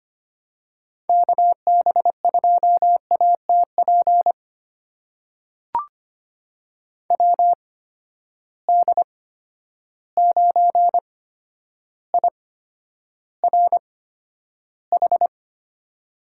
Morse code, heard as a keyed tone, about 25 wpm, spaced as for 11 wpm, 700 Hz. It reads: KB2ATP WD9IRH
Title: USA Call Signs - 6 Characters - 8x Character Spacing - 25wpm